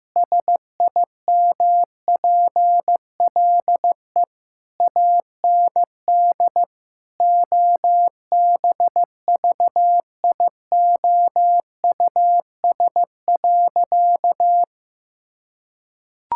Morse code, heard as a keyed tone, about 15 wpm, 700 Hz. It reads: SIMPLE AND OBVIOUS.
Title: Sentences from Top 1600 Words 15wpm